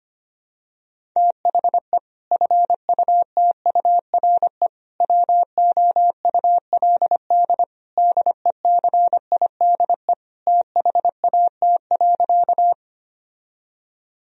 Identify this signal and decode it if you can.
Morse code, heard as a keyed tone, about 25 wpm, 700 Hz. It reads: THE FUTURE WOULD DECIDE THAT.
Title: Sentences from Top 1000 Words - Review Part 3 - 25wpm